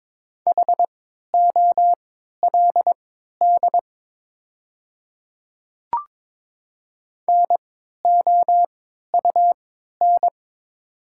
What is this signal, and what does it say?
Morse code, heard as a keyed tone, about 22 wpm, spaced as for 12 wpm, 690 Hz. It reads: HOLD NOUN
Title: 3x Character Spacing - Top 500 Words - 22wpm